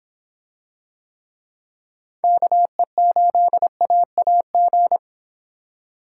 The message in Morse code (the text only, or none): KE8AAG